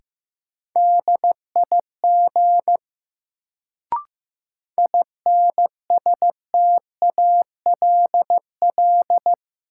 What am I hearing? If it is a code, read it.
Morse code, heard as a keyed tone, about 15 wpm, 700 Hz. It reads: DIG INSTALL
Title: Sentences from Top 1600 Words - New Word Drill Alt - 15wpm